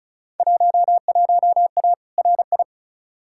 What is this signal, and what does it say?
Morse code, heard as a keyed tone, about 36 wpm, 700 Hz. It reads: 11A RI
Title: Field Day Exchanges Encore - 35wpm